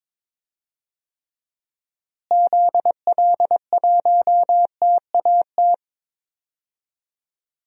Morse code, heard as a keyed tone, about 22 wpm, 700 Hz. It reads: ZL1TAT
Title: DXCC-100 Call Signs - 6 Characters - Repeated 3x - 22wpm